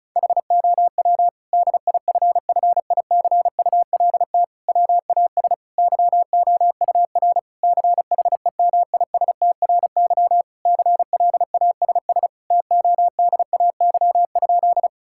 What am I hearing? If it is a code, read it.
Morse code, heard as a keyed tone, about 36 wpm, 700 Hz. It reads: HOW DIFFICULT WAS YOUR CHEMISTRY CLASS TODAY?
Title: Sentences from Top 1000 Words - Review Part 6 - 35wpm